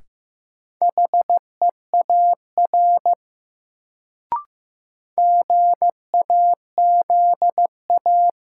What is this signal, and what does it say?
Morse code, heard as a keyed tone, about 15 wpm, 700 Hz. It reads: HEAR GAZA
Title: Sets of 5 Words - New Word Drill - 15wpm